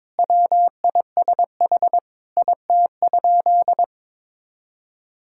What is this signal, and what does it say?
Morse code, heard as a keyed tone, about 22 wpm, 700 Hz. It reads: WISH IT?
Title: Sentences from Top 1600 Words 22wpm